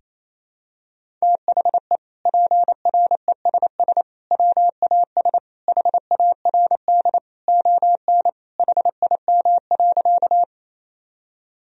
Morse code, heard as a keyed tone, about 27 wpm, 700 Hz. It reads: THE PRESS WAS HARD ON HIM.
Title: Sentences from Top 1000 Words - Review Part 1 - 28wpm